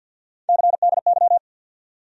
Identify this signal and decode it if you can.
Morse code, heard as a keyed tone, about 51 wpm, 700 Hz. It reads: CDY
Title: USA Call Sign Postfixes - 3 Letters 50wpm